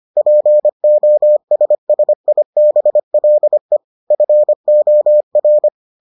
Morse code, heard as a keyed tone, about 25 wpm, 600 Hz. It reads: POSSIBLE FOR